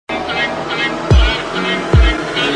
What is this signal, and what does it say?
There is no code.